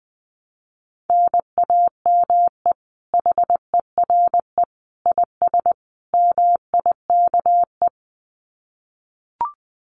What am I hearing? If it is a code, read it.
Morse code, heard as a keyed tone, about 19 wpm, 700 Hz. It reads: NAME HERE IS MIKE
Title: Common QSO Phrases 20wpm